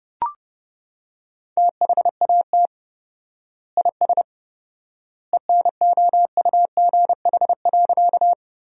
Morse code, heard as a keyed tone, about 30 wpm, 700 Hz. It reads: THAT IS ENOUGH.
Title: 4x Word-Spacing - Sentences from Top 400 Words - 30wpm